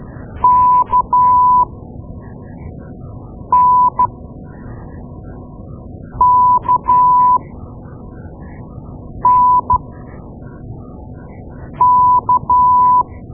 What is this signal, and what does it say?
Morse code, heard as a keyed tone, about 13 wpm, spaced as for 4 wpm, 1000 Hz. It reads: KNKNK
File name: Morse Lessons / Morse Lessons / Tape-2A